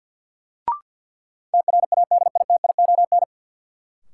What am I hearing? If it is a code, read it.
Morse code, heard as a keyed tone, about 50 wpm, 700 Hz. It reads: TRADITION
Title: Sentences from Top 1500 Words - New Word Drill - 50wpm